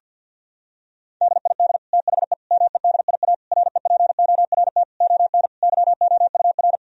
Morse code, heard as a keyed tone, about 50 wpm, 700 Hz. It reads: DID THE MEDIA REPORT ON YOUR
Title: Sentences from Top 1000 Words - Review Part 4 - 50wpm